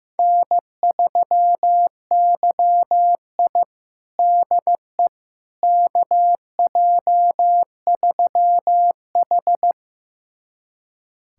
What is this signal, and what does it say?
Morse code, heard as a keyed tone, about 15 wpm, 700 Hz. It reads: N3YI DE KJ3H
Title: USA Callsign DE Callsign - 4 Characters - 15wpm